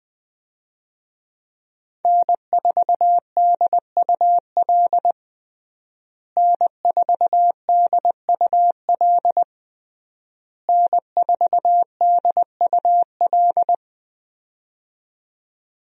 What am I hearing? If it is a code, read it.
Morse code, heard as a keyed tone, about 20 wpm, 700 Hz. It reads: N4DUL N4DUL N4DUL
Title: USA Call Signs - 5 Characters - Repeated 3x - 20wpm